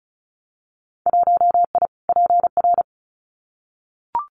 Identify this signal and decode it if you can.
Morse code, heard as a keyed tone, about 36 wpm, 700 Hz. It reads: 1I PR
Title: Winter Field Day Exchanges Encore 35wpm